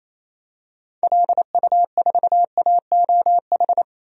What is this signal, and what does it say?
Morse code, heard as a keyed tone, about 27 wpm, 700 Hz. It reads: LU4AOH
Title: DXCC-100 Call Signs - 6 Characters - 28wpm